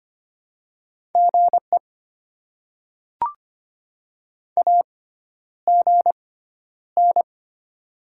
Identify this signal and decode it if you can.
Morse code, heard as a keyed tone, about 25 wpm, spaced as for 11 wpm, 700 Hz. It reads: GE AGN
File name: Top 20 QSO Elements - 6x Character Spacing - 25wpm